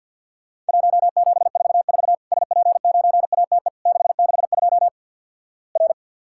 Morse code, heard as a keyed tone, about 50 wpm, 700 Hz. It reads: 1744 SP9ATE 662 R
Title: Worked All Europe DX Contest - QTC Exchange - 50wpm